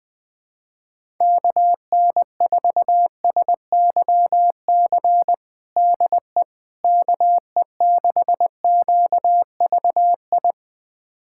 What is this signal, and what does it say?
Morse code, heard as a keyed tone, about 20 wpm, 700 Hz. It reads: KN4SYC DE KE6QVI
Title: USA Callsign DE Callsign - 6 Characters - 20wpm